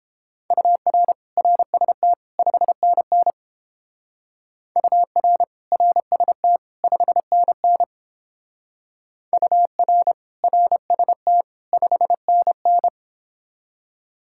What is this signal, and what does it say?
Morse code, heard as a keyed tone, about 31 wpm, 700 Hz. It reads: UR RST 5NN UR RST 5NN UR RST 5NN